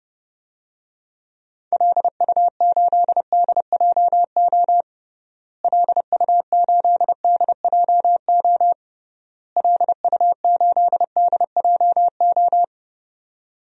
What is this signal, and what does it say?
Morse code, heard as a keyed tone, about 32 wpm, 700 Hz. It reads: LU8DJO LU8DJO LU8DJO
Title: DXCC-100 Call Signs - 6 Characters - Repeated 3x - 30wpm